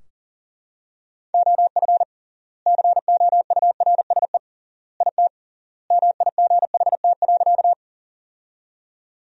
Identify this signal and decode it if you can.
Morse code, heard as a keyed tone, about 40 wpm, 700 Hz. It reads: OF COURSE IT MIGHT.
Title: 3x Word-Spacing - Sentences from Top 500 Words - 40wpm